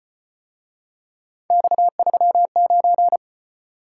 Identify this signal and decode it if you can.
Morse code, heard as a keyed tone, about 33 wpm, 700 Hz. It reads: X39